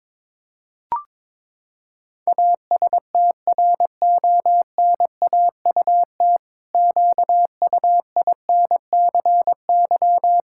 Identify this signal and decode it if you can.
Morse code, heard as a keyed tone, about 22 wpm, 700 Hz. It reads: ASTRONAUT QUINCY
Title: Pangram Challenge v2 - 22wpm